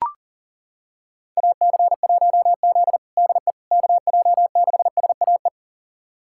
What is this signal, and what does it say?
Morse code, heard as a keyed tone, about 40 wpm, 700 Hz. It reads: AC1Z DE KJ6SAE
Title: USA Callsign DE Callsign 40wpm